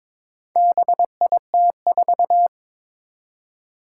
Morse code, heard as a keyed tone, about 22 wpm, 700 Hz. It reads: BIT4
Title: Four Challenging Letter-Number Combinations 22wpm